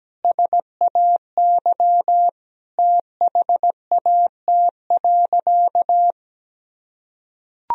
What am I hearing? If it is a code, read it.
Morse code, heard as a keyed tone, about 17 wpm, 700 Hz. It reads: SAY THAT.
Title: Sentences from Top 1000 Words - Review Part 1 - 17wpm